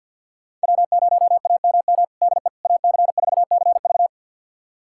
Morse code, heard as a keyed tone, about 50 wpm, 700 Hz. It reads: W0AMM DE AC4CV
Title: USA Callsign DE Callsign 50wpm